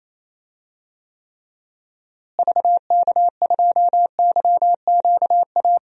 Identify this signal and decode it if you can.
Morse code, heard as a keyed tone, about 28 wpm, 700 Hz. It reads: VK2YQA
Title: DXCC-100 Call Signs - 6 Characters - Repeated 2x - 28wpm